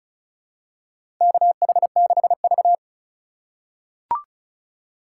Morse code, heard as a keyed tone, about 36 wpm, 700 Hz. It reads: KH6V
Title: USA Call Signs - 3x Character Spacing - 35wpm